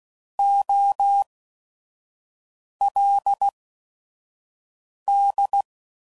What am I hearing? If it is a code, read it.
Morse code, heard as a keyed tone, about 16 wpm, spaced as for 5 wpm, 800 Hz. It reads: OLD
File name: Prefixes / By First Letter / Os